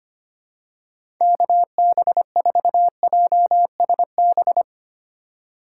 Morse code, heard as a keyed tone, about 25 wpm, 700 Hz. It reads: KB4JSB
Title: USA Call Signs - 6 Characters - 3x Character Spacing - 25wpm